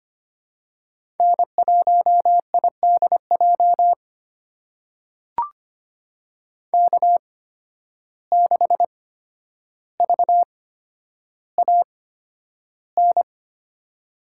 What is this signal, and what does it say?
Morse code, heard as a keyed tone, about 25 wpm, spaced as for 11 wpm, 700 Hz. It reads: N1IDJ K6VAN
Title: USA Call Signs - 5 Characters - 8x Character Spacing - 25wpm